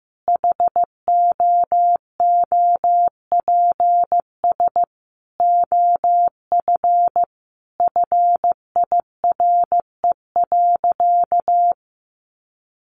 Morse code, heard as a keyed tone, about 15 wpm, 700 Hz. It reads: HOOPS OF FIRE.
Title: Easy Phrases 15wpm